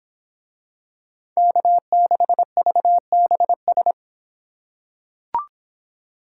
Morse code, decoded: K6VBS

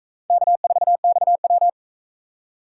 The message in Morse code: K4XW

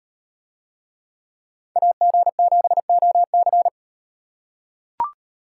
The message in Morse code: AG7OC

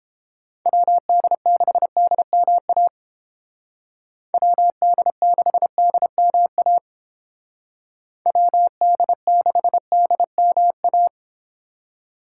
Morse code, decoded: WD6DMA WD6DMA WD6DMA

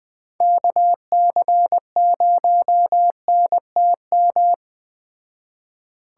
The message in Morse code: KC0NTM

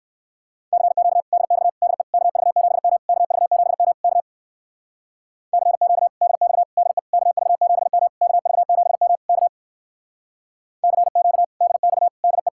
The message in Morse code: CQ DX DE KV7M KV7M K CQ DX DE KV7M KV7M K CQ DX DE

